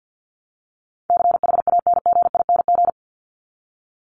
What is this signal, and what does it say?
Morse code, heard as a keyed tone, about 50 wpm, 700 Hz. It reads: CHANGING
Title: ING Suffix - Top 100 Words with Base Words 50wpm